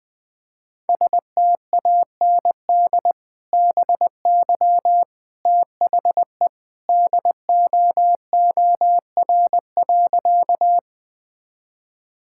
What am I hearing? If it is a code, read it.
Morse code, heard as a keyed tone, about 21 wpm, 700 Hz. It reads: STAND BY THE DOOR.
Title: Sentences with 2-Word Phrases - Spoken Beginning - 20wpm